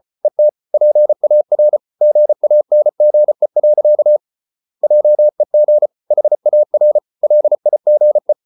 Morse code, heard as a keyed tone, about 34 wpm, 600 Hz. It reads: ET PAR GANGE. JEG HAR LIGE
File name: Ferie QRQ 34wpm